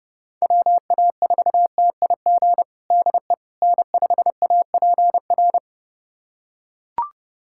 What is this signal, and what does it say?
Morse code, heard as a keyed tone, about 30 wpm, 700 Hz. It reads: WA4TIG DE N5APR
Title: USA Callsign DE Callsign 30wpm